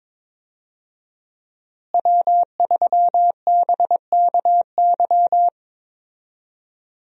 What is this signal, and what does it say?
Morse code, heard as a keyed tone, about 22 wpm, 700 Hz. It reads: W3BKY
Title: USA Call Signs - 5 Characters - Repeated 2x - 22wpm